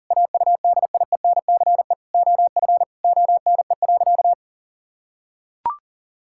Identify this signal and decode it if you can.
Morse code, heard as a keyed tone, about 37 wpm, 700 Hz. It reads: AUDIENCE OF ONE.
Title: Sentences from Top 1000 Words - Review Part 5 - 40wpm